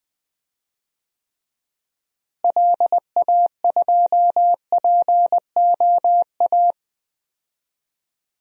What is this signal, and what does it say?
Morse code, heard as a keyed tone, about 21 wpm, 700 Hz. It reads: LA2POA
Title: DXCC-100 Call Signs - 6 Characters - Repeated 2x - 20wpm